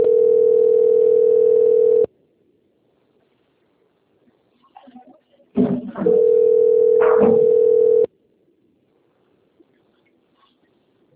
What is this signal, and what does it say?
Ringback tone (North America), 440 + 480 Hz.